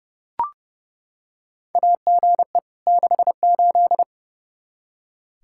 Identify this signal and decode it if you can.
Morse code, heard as a keyed tone, about 30 wpm, 700 Hz. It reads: AGE 68
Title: Practice QSO - Exchange 3 - More Info - 30wpm